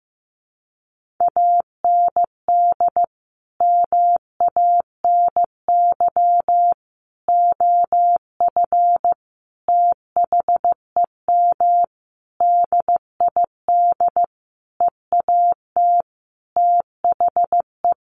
Morse code, heard as a keyed tone, about 15 wpm, 700 Hz. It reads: AND MANY OF THEM DID EAT THE
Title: Sentences from Top 1500 Words 15wpm